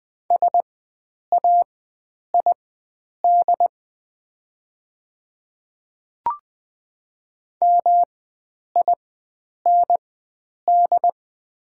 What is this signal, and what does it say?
Morse code, heard as a keyed tone, about 20 wpm, spaced as for 9 wpm, 700 Hz. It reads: SAID MIND
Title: 4x Character Spacing - Top 500 Words - 20wpm